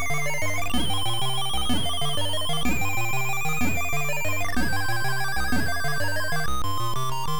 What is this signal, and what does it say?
There is no code.